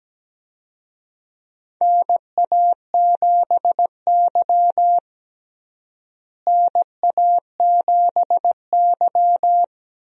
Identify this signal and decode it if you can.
Morse code, heard as a keyed tone, about 17 wpm, 700 Hz. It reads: NA7Y NA7Y
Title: USA Call Signs - 4 Characters - Repeated 3x - 17wpm